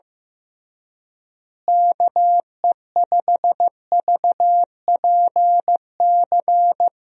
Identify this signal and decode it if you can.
Morse code, heard as a keyed tone, about 15 wpm, 700 Hz. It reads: KE5VPC
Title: USA Call Signs - 6 Characters - Repeated 3x - 15wpm